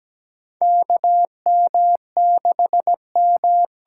Morse code, heard as a keyed tone, about 17 wpm, 700 Hz. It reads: K M 6 M
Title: USA Call Signs - 6 Characters - 8x Character Spacing - 17wpm